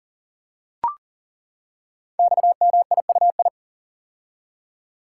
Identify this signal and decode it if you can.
Morse code, heard as a keyed tone, about 40 wpm, 700 Hz. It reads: XMIUI
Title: Code Groups - 5 Letters - 40wpm